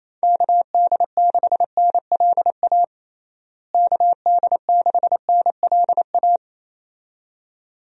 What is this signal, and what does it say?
Morse code, heard as a keyed tone, about 28 wpm, 700 Hz. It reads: KD6NLA KD6NLA
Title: USA Call Signs - 6 Characters - Repeated 2x - 28wpm